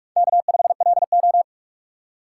Morse code, heard as a keyed tone, about 45 wpm, 700 Hz. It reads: K5LO
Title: USA Call Signs - 4 Characters - 4x Character Spacing - 45wpm